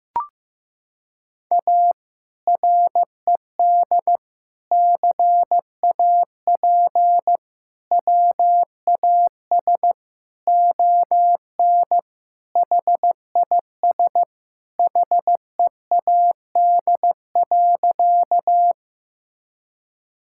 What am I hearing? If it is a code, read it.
Morse code, heard as a keyed tone, about 15 wpm, 700 Hz. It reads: A RED CAP WAS ON HIS HEAD.
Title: Sentences from Top 1600 Words 15wpm